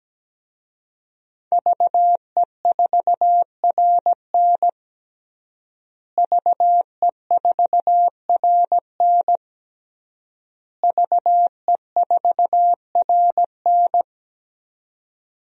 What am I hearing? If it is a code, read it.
Morse code, heard as a keyed tone, about 17 wpm, 700 Hz. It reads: VE4RN VE4RN VE4RN
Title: DXCC-100 Call Signs - 5 Characters - Repeated 3x - 17wpm